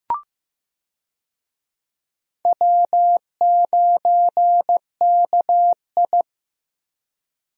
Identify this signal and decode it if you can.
Morse code, heard as a keyed tone, about 15 wpm, 700 Hz. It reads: W9KI
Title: USA Call Signs - 4 Characters - Repeated 3x - 15wpm